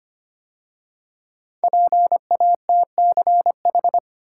Morse code, heard as a keyed tone, about 25 wpm, 700 Hz. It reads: PATCH
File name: 5 Letter Words - First 3 Letters Given - 25wpm